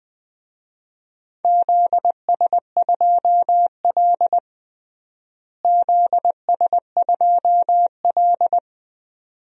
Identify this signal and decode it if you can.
Morse code, heard as a keyed tone, about 21 wpm, 700 Hz. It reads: ZS2L ZS2L
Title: DXCC-100 Call Signs - 4 Characters - Repeated 2x - 20wpm